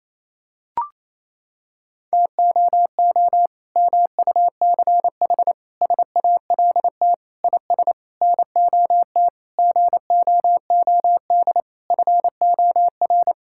Morse code, heard as a keyed tone, about 28 wpm, 700 Hz. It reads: TOO MUCH SALT IS NOT GOOD FOR
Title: Sentences from Top 1000 Words - Review Part 5 - 28wpm